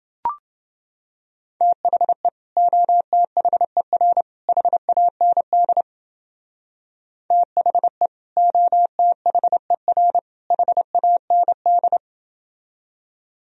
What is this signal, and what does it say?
Morse code, heard as a keyed tone, about 32 wpm, 700 Hz. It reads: THE OTHER HAND THE OTHER HAND